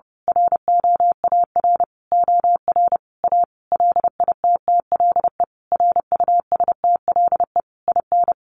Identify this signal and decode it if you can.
Morse code, heard as a keyed tone, about 30 wpm, 700 Hz. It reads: ROAR OR A LITTLE RUSTLE IN
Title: A Sound Of Thunder 30wpm